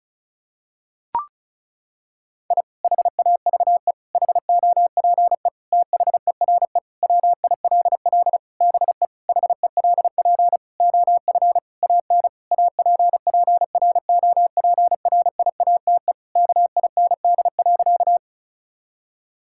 Morse code, heard as a keyed tone, about 36 wpm, 700 Hz. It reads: I HAVE HOPE THERE WILL BE HELP OF AN APPROPRIATE KIND.
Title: Sentences from Top 1000 Words - Review Part 5 - 35wpm